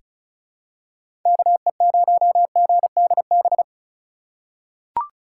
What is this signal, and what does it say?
Morse code, heard as a keyed tone, about 36 wpm, 700 Hz. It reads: KE0GDB
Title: USA Call Signs - 6 Characters - 3x Character Spacing - 35wpm